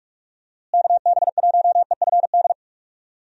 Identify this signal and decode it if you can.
Morse code, heard as a keyed tone, about 46 wpm, 700 Hz. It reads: KB1EFD
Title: USA Call Signs - 6 Characters - Repeated 2x - 45wpm